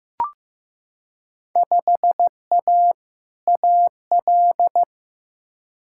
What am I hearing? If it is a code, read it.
Morse code, heard as a keyed tone, about 15 wpm, 700 Hz. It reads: 5A AL
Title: Field Day Exchanges Encore - 15wpm